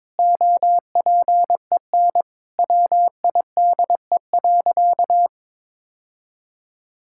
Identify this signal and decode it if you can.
Morse code, heard as a keyed tone, about 22 wpm, 700 Hz. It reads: OPEN WIDE.
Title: Sentences from Top 1000 Words - Review Part 6 - 22wpm